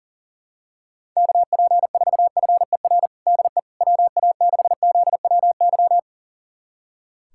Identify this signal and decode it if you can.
Morse code, heard as a keyed tone, about 37 wpm, 700 Hz. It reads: KP4FER DE WA6ZWY
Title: USA Callsign DE Callsign - 6 Characters - 40wpm